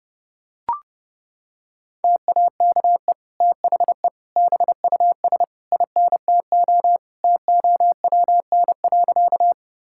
Morse code, heard as a keyed tone, about 30 wpm, 700 Hz. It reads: TAKE THE BUS INTO TOWN.
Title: Sentences from Top 1000 Words - Review Part 6 - 30wpm